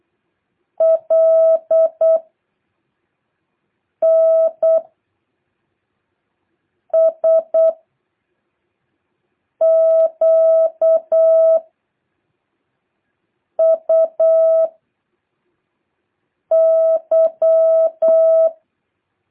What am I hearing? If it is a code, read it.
Morse code, heard as a keyed tone, about 8 wpm, 650 Hz. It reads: L N S Q U Y
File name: jh8cbh / 2023cw